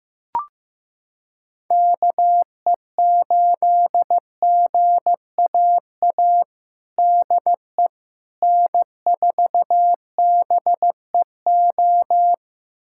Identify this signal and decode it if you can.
Morse code, heard as a keyed tone, about 15 wpm, 700 Hz. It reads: KE8GAA DE N4BEO
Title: USA Callsign DE Callsign 15wpm